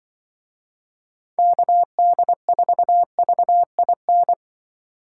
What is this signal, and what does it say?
Morse code, heard as a keyed tone, about 24 wpm, 700 Hz. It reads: KD4VIN
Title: Speed Racing - USA 2x3 Call Signs 20wpm